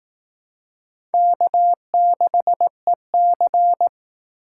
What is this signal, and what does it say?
Morse code, heard as a keyed tone, about 19 wpm, 700 Hz. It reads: K6EC